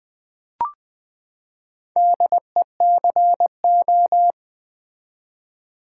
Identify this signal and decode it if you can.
Morse code, heard as a keyed tone, about 20 wpm, 700 Hz. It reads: DECO
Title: Build A Word - 4 Letters At A Time - 8-Letter Words - 20wpm